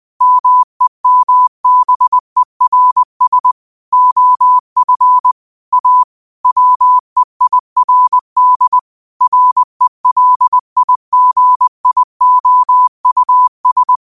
Morse code, heard as a keyed tone, about 20 wpm, 1000 Hz. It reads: MEMBERS OF A WEIRD RELIGIOUS
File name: qotdfiles / 20wpm